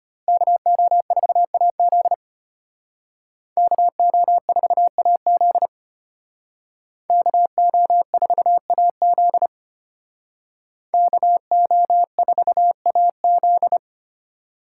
Morse code, heard as a keyed tone, about 31 wpm, 700 Hz. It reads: KO4AZ KO4AZ KO4AZ KO4AZ